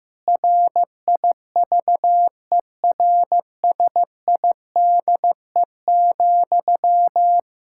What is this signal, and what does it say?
Morse code, heard as a keyed tone, about 15 wpm, 700 Hz. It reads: RIVERSIDE,
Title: K1USN SST QSO Practice 15wpm